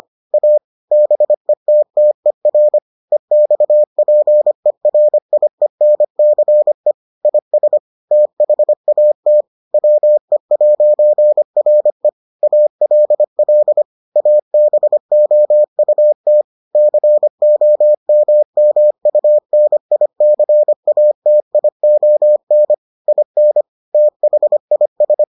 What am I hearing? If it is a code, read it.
Morse code, heard as a keyed tone, about 25 wpm, 600 Hz. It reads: A BETTER EXPERIENCE IS THAT WE'RE ALL ABOUT COMMUNICATION IN THIS